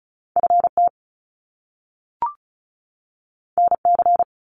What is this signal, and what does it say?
Morse code, heard as a keyed tone, about 35 wpm, 700 Hz. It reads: FT NC
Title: Top Two Letter Combinations in Words 35wpm